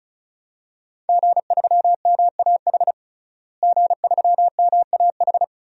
Morse code, heard as a keyed tone, about 36 wpm, 700 Hz. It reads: G3MAH G3MAH
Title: DXCC-100 Call Signs - 5 Characters - Repeated 2x - 35wpm